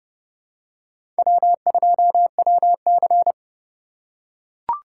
Morse code, heard as a keyed tone, about 30 wpm, 700 Hz. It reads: W2WC
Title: USA Call Signs - 4 Characters - 2x Character Spacing - 30wpm